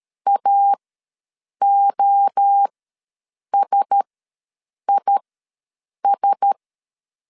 Morse code, heard as a keyed tone, about 13 wpm, spaced as for 7 wpm, 800 Hz. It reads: AOSIS
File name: LEZIONI / 5 LBVC / 5 LB ws 12wpm 3034